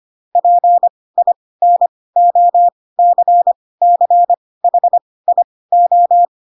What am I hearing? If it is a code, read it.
Morse code, heard as a keyed tone, about 25 wpm, spaced as for 18 wpm, 700 Hz. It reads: PINOCCHIO
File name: Pinocchio / Pinocchio ch4 12 25 ik1hge